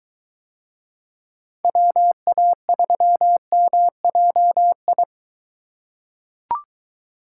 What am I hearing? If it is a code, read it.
Morse code, heard as a keyed tone, about 23 wpm, 700 Hz. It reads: WA3MJI